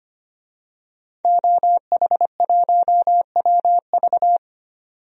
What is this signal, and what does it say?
Morse code, heard as a keyed tone, about 25 wpm, 700 Hz. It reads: OH1WV
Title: DXCC-100 Call Signs - 5 Characters - 25wpm